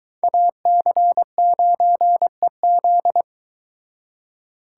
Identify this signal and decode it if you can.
Morse code, heard as a keyed tone, about 23 wpm, 700 Hz. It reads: AC9EZ